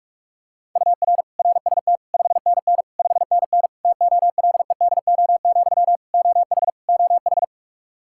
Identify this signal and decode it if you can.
Morse code, heard as a keyed tone, about 46 wpm, 700 Hz. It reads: UR RST 5NN 5NN TOLEDO, OH OH
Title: Practice QSO - Exchange 1 - Beginning - 45wpm